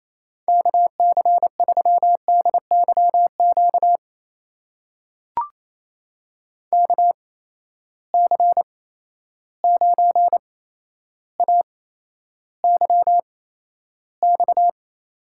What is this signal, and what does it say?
Morse code, heard as a keyed tone, about 28 wpm, spaced as for 12 wpm, 700 Hz. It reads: KC3DYQ KC9AYX
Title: USA Call Signs - 6 Characters - 8x Character Spacing - 28wpm